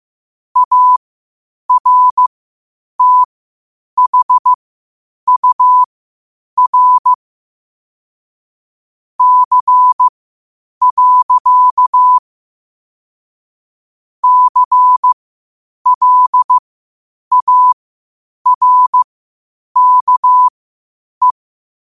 Morse code, heard as a keyed tone, about 15 wpm, spaced as for 8 wpm, 1000 Hz. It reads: ARTHUR C. CLARKE
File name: qotdfiles / 8wpm